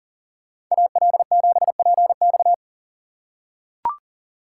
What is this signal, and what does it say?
Morse code, heard as a keyed tone, about 39 wpm, 700 Hz. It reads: AL7PX
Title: USA Call Signs - 5 Characters - 6x Character Spacing - 40wpm